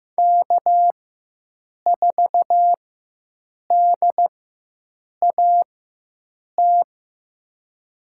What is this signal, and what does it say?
Morse code, heard as a keyed tone, about 15 wpm, spaced as for 7 wpm, 700 Hz. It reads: K4DAT
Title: USA Call Signs - 5 Characters - 4x Character Spacing - 15wpm